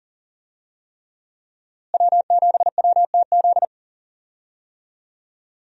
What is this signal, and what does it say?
Morse code, heard as a keyed tone, about 40 wpm, 700 Hz. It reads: W7WTZ
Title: USA Call Signs - 5 Characters - Repeated 3x - 40wpm